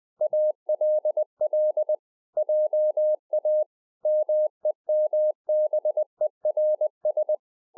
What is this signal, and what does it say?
Morse code, heard as a keyed tone, about 20 wpm, 610 Hz. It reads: ALL JA MEMBERS